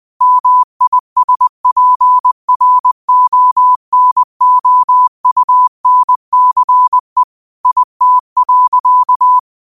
Morse code, heard as a keyed tone, about 20 wpm, 1000 Hz. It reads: MISPRONOUNCE IT.
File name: qotdfiles / 20wpm